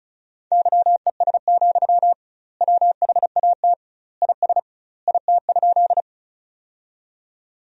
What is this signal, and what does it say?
Morse code, heard as a keyed tone, about 36 wpm, 700 Hz. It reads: YES, WHAT IS IT?
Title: 2x Word-Spacing - Sentences from Top 500 Words - 35wpm